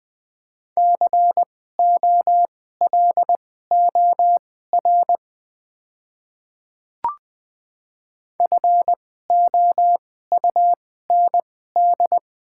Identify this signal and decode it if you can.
Morse code, heard as a keyed tone, about 20 wpm, spaced as for 14 wpm, 700 Hz. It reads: COLOR FOUND
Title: 2x Character Spacing - Top 300 Words - 20wpm